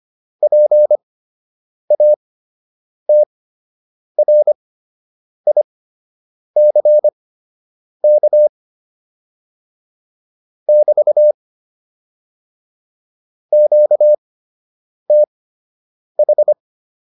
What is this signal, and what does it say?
Morse code, heard as a keyed tone, about 25 wpm, spaced as for 8 wpm, 600 Hz. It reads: PATRICK <BT> QTH